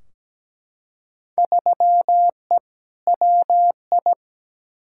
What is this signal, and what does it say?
Morse code, heard as a keyed tone, about 17 wpm, 700 Hz. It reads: 3E WI
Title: Field Day Exchanges Encore - 17wpm